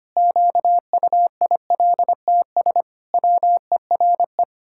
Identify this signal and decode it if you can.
Morse code, heard as a keyed tone, about 25 wpm, 700 Hz. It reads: QUILTS WERE
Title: Pangram Challenge v2 - 25wpm